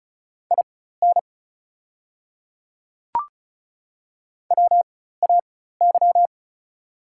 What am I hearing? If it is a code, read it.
Morse code, heard as a keyed tone, about 36 wpm, spaced as for 16 wpm, 700 Hz. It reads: IN WAY
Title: 4x Character Spacing - Top 100 Words - 35wpm